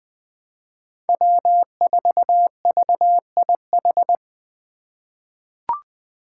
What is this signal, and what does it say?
Morse code, heard as a keyed tone, about 20 wpm, 700 Hz. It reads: W4VIH
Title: Code Groups - 5 Letter:Numbers - 20wpm